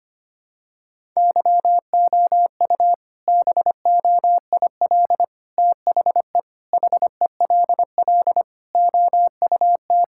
Morse code, heard as a keyed tone, about 25 wpm, 700 Hz. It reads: YOU BOIL THE HELL OUT
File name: practice / 25wpm / All-Age Appropriate Jokes - 25wpm